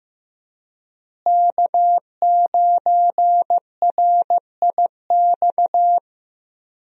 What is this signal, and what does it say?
Morse code, heard as a keyed tone, about 15 wpm, 700 Hz. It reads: K9RIX
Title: USA Call Signs - 5 Characters - Repeated 2x - 15wpm